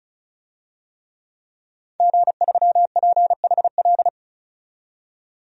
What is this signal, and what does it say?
Morse code, heard as a keyed tone, about 36 wpm, 700 Hz. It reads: G3PHL
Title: DXCC-100 Call Signs - 5 Characters - 35wpm